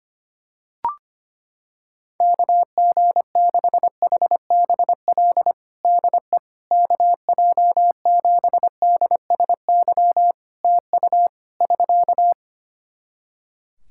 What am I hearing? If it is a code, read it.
Morse code, heard as a keyed tone, about 25 wpm, 700 Hz. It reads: KG6HBL DE KJ7DSY TU <SK>
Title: K1USN SST QSO Practice 25wpm